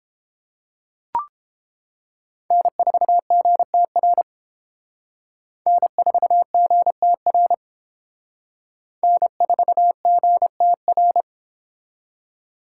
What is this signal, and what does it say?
Morse code, heard as a keyed tone, about 28 wpm, 700 Hz. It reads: N4GTR N4GTR N4GTR